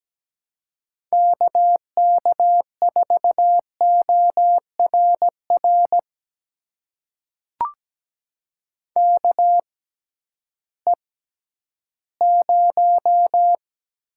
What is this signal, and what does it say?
Morse code, heard as a keyed tone, about 17 wpm, spaced as for 11 wpm, 700 Hz. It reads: KK4ORR KE0
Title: USA Call Signs - 6 Characters - 6x Character Spacing - 17wpm